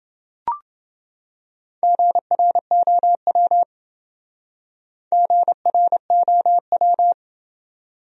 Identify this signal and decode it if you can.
Morse code, heard as a keyed tone, about 27 wpm, 700 Hz. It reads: GROW GROW